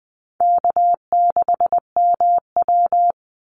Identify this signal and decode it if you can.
Morse code, heard as a keyed tone, about 20 wpm, 690 Hz. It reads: K6MW
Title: NA SOTA Chaser Callsigns 20wpm